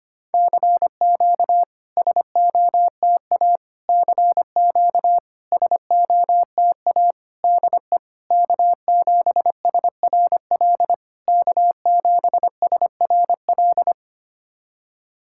Morse code, heard as a keyed tone, about 25 wpm, 690 Hz. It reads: CQ SOTA CQ SOTA DE K7SRL K7SRL
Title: SOTA QSO Practice 25wpm